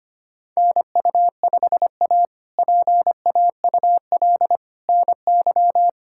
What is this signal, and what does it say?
Morse code, heard as a keyed tone, about 25 wpm, 700 Hz. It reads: NU5A PAUL NY
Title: K1USN SST QSO Practice - 25wpm